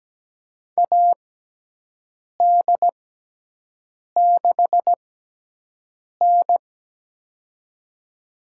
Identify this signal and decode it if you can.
Morse code, heard as a keyed tone, about 17 wpm, spaced as for 6 wpm, 700 Hz. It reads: AD6N
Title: USA Call Signs - 4 Characters - 6x Character Spacing - 17wpm